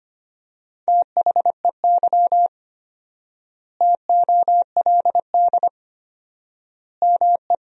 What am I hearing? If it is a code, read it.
Morse code, heard as a keyed tone, about 25 wpm, 700 Hz. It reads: THEY TOLD ME
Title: 4x Word-Spacing - Sentences from Top 400 Words - 25wpm